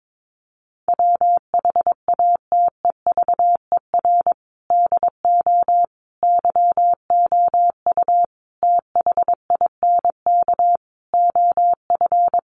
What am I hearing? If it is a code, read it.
Morse code, heard as a keyed tone, about 22 wpm, 700 Hz. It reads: WHATEVER DO YOU THINK OF
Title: Sentences from Top 1100 Words 22wpm